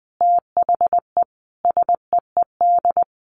Morse code, heard as a keyed tone, about 20 wpm, 700 Hz. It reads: THE SEED
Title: A Sound Of Thunder 20wpm